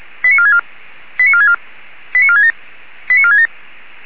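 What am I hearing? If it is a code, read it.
Five-tone selective call (ZVEI), heard as quick runs of five tones, tones of about 70 ms. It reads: 89464 89464 89467 89467
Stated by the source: Received on a software-defined radio (SDR); ZVEI five-tone selective call 89299.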